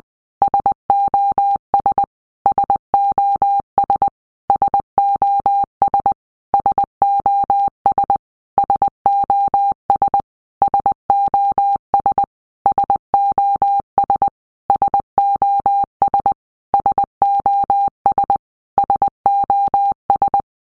Morse code, heard as a keyed tone, about 20 wpm, 800 Hz. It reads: SOS SOS SOS SOS SOS SOS SOS SOS SOS SOS
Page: SOS sound in Morse Code (Medium)